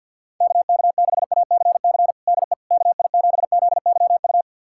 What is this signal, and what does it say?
Morse code, heard as a keyed tone, about 51 wpm, 690 Hz. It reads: KK6ACC DE KI7ZYU
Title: USA Callsign DE Callsign - 6 Characters - 50wpm